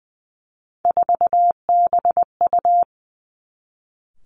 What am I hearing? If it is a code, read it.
Morse code, heard as a keyed tone, about 20 wpm, 700 Hz. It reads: 4BU
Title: Three Challenging Letter-Number Combinations 20wpm